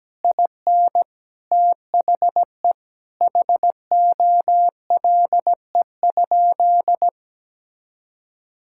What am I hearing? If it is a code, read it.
Morse code, heard as a keyed tone, about 17 wpm, 700 Hz. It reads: IN THE HOLE?
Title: Sentences from Top 1600 Words 17wpm